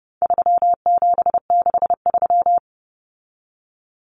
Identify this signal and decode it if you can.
Morse code, heard as a keyed tone, about 30 wpm, 700 Hz. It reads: 3763
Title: Common Cut Numbers - 4-Digit 30wpm